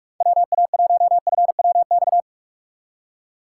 Morse code, heard as a keyed tone, about 43 wpm, 700 Hz. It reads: WA1FWX